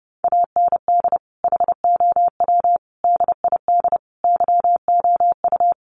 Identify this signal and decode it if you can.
Morse code, heard as a keyed tone, about 30 wpm, 700 Hz. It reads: AND HOW DID YOU
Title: Sentences from Top 1400 Words 30wpm